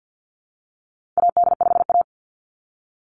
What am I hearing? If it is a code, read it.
Morse code, heard as a keyed tone, about 48 wpm, 700 Hz. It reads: AD5A
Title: Call Signs Repeated 3x - 50wpm